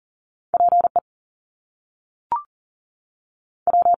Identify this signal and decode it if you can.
Morse code, heard as a keyed tone, about 40 wpm, 700 Hz. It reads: PE W
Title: Top Two Letter Combinations in Words 40wpm